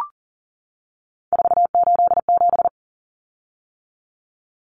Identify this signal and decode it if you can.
Morse code, heard as a keyed tone, about 40 wpm, 700 Hz. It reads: 487
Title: Common Cut Numbers - 3-Digit 40wpm